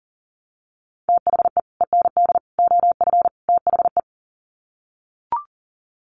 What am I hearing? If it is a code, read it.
Morse code, heard as a keyed tone, about 40 wpm, 700 Hz. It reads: THE END OF THE
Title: Sets of 4 Words 40wpm